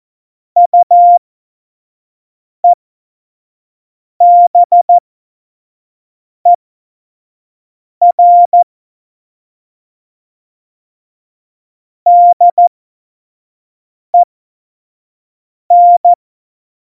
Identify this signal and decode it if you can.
Morse code, heard as a keyed tone, about 14 wpm, spaced as for 5 wpm, 700 Hz. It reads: UEBER DEN
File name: Morsekurs / Witze / Witz 29. Dezember 2020, 25 F70